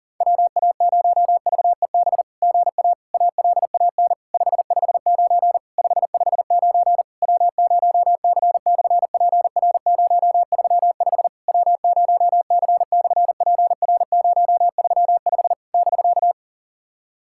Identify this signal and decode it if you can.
Morse code, heard as a keyed tone, about 40 wpm, 690 Hz. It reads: WA0VEB GA ALAN 559 559 W0C/PR035 W0C/PR035 <BK>
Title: SOTA QSO Practice 40wpm